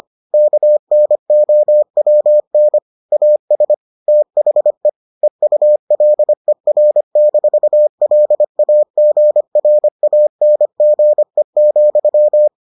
Morse code, heard as a keyed tone, about 25 wpm, 600 Hz. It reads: KNOWN AS THE EULER-LAGRANGE,